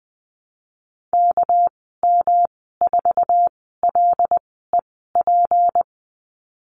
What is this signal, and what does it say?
Morse code, heard as a keyed tone, about 20 wpm, spaced as for 14 wpm, 700 Hz. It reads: KM4LEP
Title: 200 Lesson 32g - Letter k - Call Signs 20wpm